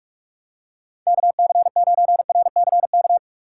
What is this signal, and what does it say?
Morse code, heard as a keyed tone, about 44 wpm, 700 Hz. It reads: KC9RCK
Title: USA Call Signs - 6 Characters - 2x Character Spacing - 45wpm